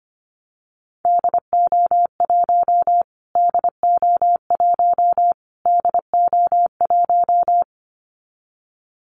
Morse code, heard as a keyed tone, about 25 wpm, 700 Hz. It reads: DO1 DO1 DO1
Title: Callsign Prefixes for Top 25 DXCC - Repeated 3x - 25wpm